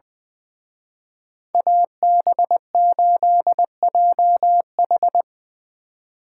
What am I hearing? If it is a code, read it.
Morse code, heard as a keyed tone, about 24 wpm, 700 Hz. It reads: AB8JH
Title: Speed Racing - USA 2x2 Call Signs 20wpm